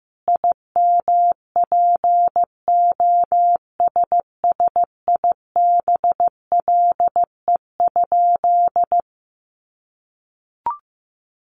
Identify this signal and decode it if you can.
Morse code, heard as a keyed tone, about 15 wpm, 700 Hz. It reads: IMPOSSIBLE?
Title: Sentences from Top 1100 Words 15wpm